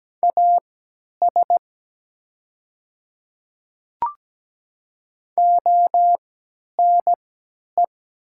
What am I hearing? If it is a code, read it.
Morse code, heard as a keyed tone, about 17 wpm, spaced as for 10 wpm, 700 Hz. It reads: AS ONE
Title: 3x Character Spacing - Top 200 Words - 17wpm